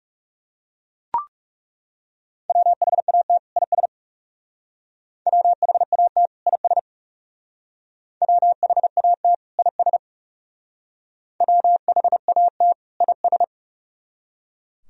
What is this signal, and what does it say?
Morse code, heard as a keyed tone, about 36 wpm, 700 Hz. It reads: WHAT IS WHAT IS WHAT IS WHAT IS
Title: Speed-Racing - Sets of 2 Words 30wpm